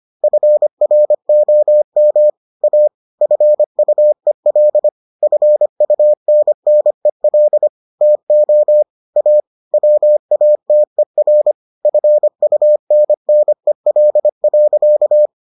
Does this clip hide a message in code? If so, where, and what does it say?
Yes: Morse code from 0.2 to 15.3 s, heard as a keyed tone, about 25 wpm, 600 Hz. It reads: FROM A FUEL FUNNEL TO A WATER FUNNEL.